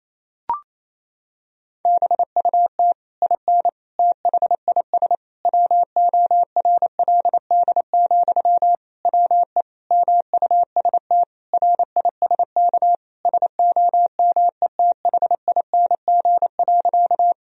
Morse code, heard as a keyed tone, about 28 wpm, 700 Hz. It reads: BUT IN THIS WORLD, WE MUST RISK SOMETHING.
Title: Sentences from Top 1000 Words - Review Part 4 - 28wpm